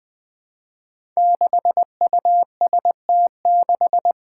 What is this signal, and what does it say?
Morse code, heard as a keyed tone, about 21 wpm, 700 Hz. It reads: 6UST6
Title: Five Challenging Letter-Number Combinations 20wpm